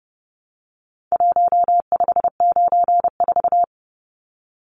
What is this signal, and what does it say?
Morse code, heard as a keyed tone, about 30 wpm, 700 Hz. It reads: 1594
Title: Numbers - Contest Style - 4-Digit 30wpm